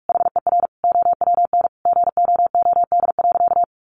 Morse code, heard as a keyed tone, about 44 wpm, 700 Hz. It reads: HER OWN GOOD.
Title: Sentences from Top 1300 Words 45wpm